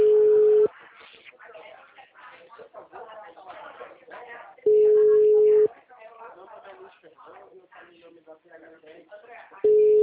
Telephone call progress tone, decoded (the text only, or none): ringback tone